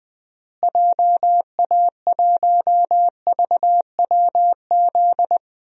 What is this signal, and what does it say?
Morse code, heard as a keyed tone, about 20 wpm, 700 Hz. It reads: JA1VWZ
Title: DXCC-100 Call Signs - 6 Characters - Repeated 3x - 20wpm